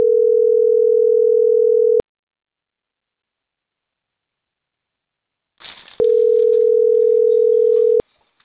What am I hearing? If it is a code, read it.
Ringback tone (North America), 440 + 480 Hz.